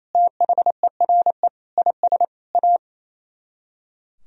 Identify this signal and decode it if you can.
Morse code, heard as a keyed tone, about 36 wpm, 700 Hz. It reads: THERE IS A